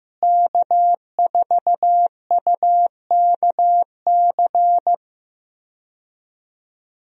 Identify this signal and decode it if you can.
Morse code, heard as a keyed tone, about 15 wpm, 700 Hz. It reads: K4UKC
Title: USA Call Signs - 5 Characters - Repeated 3x - 15wpm